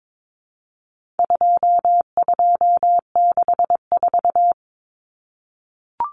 Morse code, heard as a keyed tone, about 22 wpm, 700 Hz. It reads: 2264
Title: Numbers - 4-Digit 22wpm